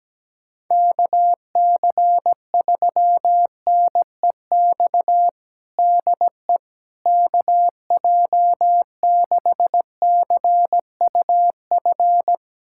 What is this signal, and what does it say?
Morse code, heard as a keyed tone, about 17 wpm, 700 Hz. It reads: KC3NEX DE KJ6CUF
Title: USA Callsign DE Callsign - 6 Characters - 17wpm